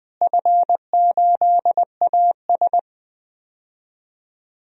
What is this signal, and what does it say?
Morse code, heard as a keyed tone, about 20 wpm, 700 Hz. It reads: F8AS
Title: DXCC-100 Call Signs - 4 Characters - Repeated 3x - 20wpm